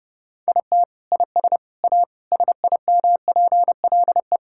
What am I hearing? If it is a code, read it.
Morse code, heard as a keyed tone, about 30 wpm, 700 Hz. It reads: IT IS A SIMPLE
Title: Sentences from Top 1000 Words - Review Part 6 - 30wpm